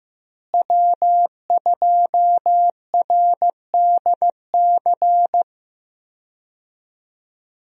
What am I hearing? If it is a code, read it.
Morse code, heard as a keyed tone, about 16 wpm, 700 Hz. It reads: W2RDC
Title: USA Call Signs - 5 Characters - Repeated 3x - 15wpm